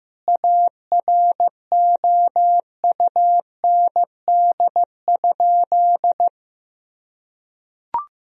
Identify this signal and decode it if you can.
Morse code, heard as a keyed tone, about 16 wpm, 700 Hz. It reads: AROUND?
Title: Sentences from Top 1000 Words - Review Part 4 - 15wpm